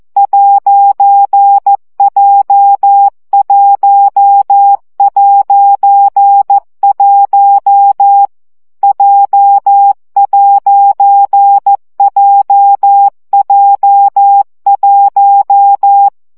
Morse code, heard as a keyed tone, about 14 wpm, 800 Hz. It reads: 'J1'1 J'JJ1
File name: CW gyakorlas j 1 aposztrof 12 baud